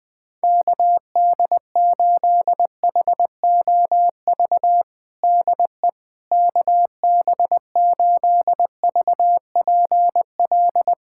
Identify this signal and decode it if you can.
Morse code, heard as a keyed tone, about 21 wpm, 700 Hz. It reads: KD8HOV DE KB8VPL
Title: USA Callsign DE Callsign - 6 Characters - 20wpm